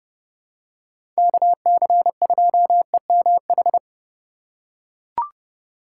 Morse code, heard as a keyed tone, about 32 wpm, 700 Hz. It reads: KC2EMH
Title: USA Call Signs - 6 Characters - 4x Character Spacing - 30wpm